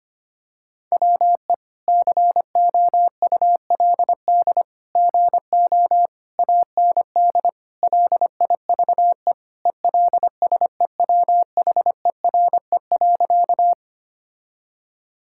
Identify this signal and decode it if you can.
Morse code, heard as a keyed tone, about 25 wpm, 700 Hz. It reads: WE COULD GO AND LIVE ELSEWHERE.
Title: Sentences from Top 1600 Words 25wpm